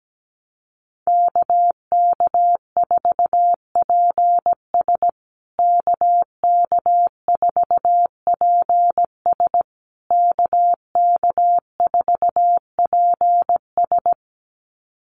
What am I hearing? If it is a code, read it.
Morse code, heard as a keyed tone, about 17 wpm, 700 Hz. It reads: KK4PS KK4PS KK4PS
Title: USA Call Signs Repeated 3x 17wpm